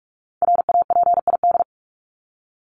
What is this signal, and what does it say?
Morse code, heard as a keyed tone, about 44 wpm, 700 Hz. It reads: RAPID
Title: LY Suffix - Top 100 Words with Base Words 45wpm